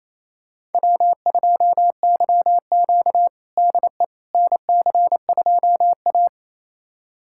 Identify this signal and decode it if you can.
Morse code, heard as a keyed tone, about 27 wpm, 700 Hz. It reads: W2YQ DE NC2A
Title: USA Callsign DE Callsign - 4 Characters - 28wpm